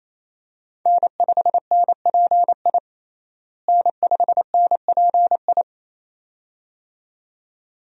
Morse code, heard as a keyed tone, about 28 wpm, 700 Hz. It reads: N5NPI N5NPI
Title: USA Call Signs - 5 Characters - Repeated 2x - 28wpm